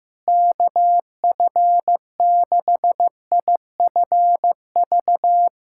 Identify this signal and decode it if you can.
Morse code, heard as a keyed tone, about 15 wpm, 700 Hz. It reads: KF6IFV
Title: USA Callsign DE Callsign - 6 Characters - 15wpm